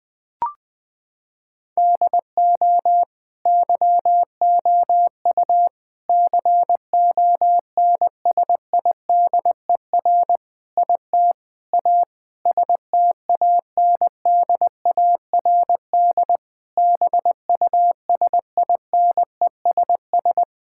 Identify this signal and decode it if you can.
Morse code, heard as a keyed tone, about 20 wpm, 700 Hz. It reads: DO YOU CONSIDER IT A STANDARD BUSINESS
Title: Sentences from Top 1000 Words - Review Part 6 - 20wpm